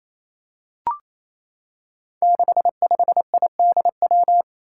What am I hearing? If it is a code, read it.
Morse code, heard as a keyed tone, about 28 wpm, 700 Hz. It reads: 65IDW
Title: Five Challenging Letter-Number Combinations 28wpm